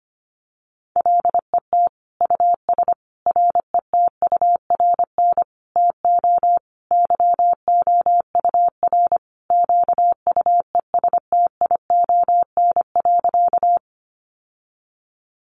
Morse code, heard as a keyed tone, about 25 wpm, 700 Hz. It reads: LET US RETURN TO YOUR QUESTION.
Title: Sentences from Top 800 Words 25wpm